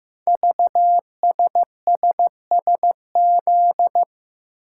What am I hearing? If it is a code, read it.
Morse code, heard as a keyed tone, about 15 wpm, 700 Hz. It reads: VSSSZ
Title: Code Groups - 5 Letters - 15wpm